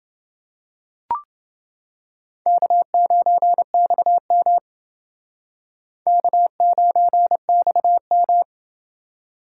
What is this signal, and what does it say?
Morse code, heard as a keyed tone, about 29 wpm, 700 Hz. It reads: K9XM K9XM